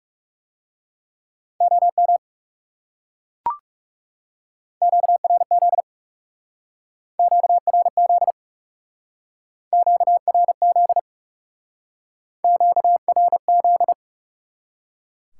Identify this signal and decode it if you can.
Morse code, heard as a keyed tone, about 37 wpm, 700 Hz. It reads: OM QRZ QRZ QRZ QRZ
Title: Speed-Racing Top 90 QSO Elements 30wpm